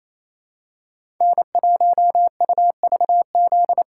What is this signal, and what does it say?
Morse code, heard as a keyed tone, about 28 wpm, 700 Hz. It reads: N1UVZ
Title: USA Call Signs - 3x Character Spacing - 28wpm